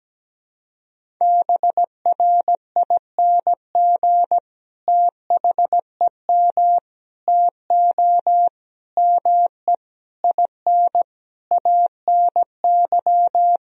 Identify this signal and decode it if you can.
Morse code, heard as a keyed tone, about 17 wpm, 700 Hz. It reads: BRING THEM TO ME IN ANY
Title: Sentences from Top 1000 Words - Review Part 3 - 17wpm